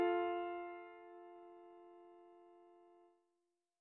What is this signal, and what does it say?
Dial tone (North America), 350 + 440 Hz.